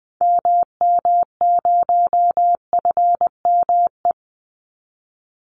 Morse code, heard as a keyed tone, about 20 wpm, 700 Hz. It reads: MM0FME
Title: Call Signs - 20wpm